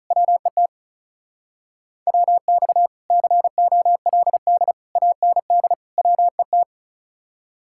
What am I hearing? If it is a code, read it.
Morse code, heard as a keyed tone, about 43 wpm, 700 Hz. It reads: WET WX COLD AND WET
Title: Speed-Racing Common QSO Phrases Encore - 35wpm